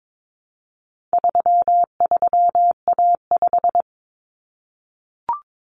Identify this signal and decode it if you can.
Morse code, heard as a keyed tone, about 22 wpm, 700 Hz. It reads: 33A5
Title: Common Cut Numbers - 4-Digit 22wpm